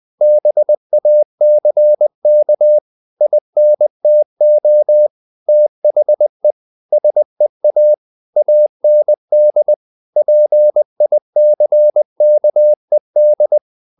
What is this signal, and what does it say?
Morse code, heard as a keyed tone, about 20 wpm, 600 Hz. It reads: BACK INTO THE SEA AND PICKED
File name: the old man and the sea part 2 20wpm